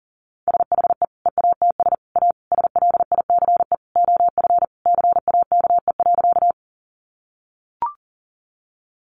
Morse code, heard as a keyed tone, about 40 wpm, 700 Hz. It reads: SHE EATS A SLICE OF CAKE.
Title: Additional Easy Phrases 40wpm